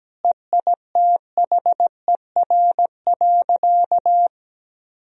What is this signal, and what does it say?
Morse code, heard as a keyed tone, about 17 wpm, 700 Hz. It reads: EITHER.
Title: Sentences from Top 1000 Words - Review Part 4 - 17wpm